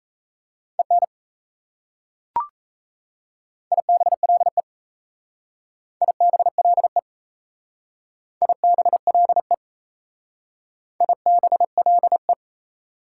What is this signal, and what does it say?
Morse code, heard as a keyed tone, about 34 wpm, 690 Hz. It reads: EN IBLE IBLE IBLE IBLE
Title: Speed-Racing Prefixes and Suffixes 28wpm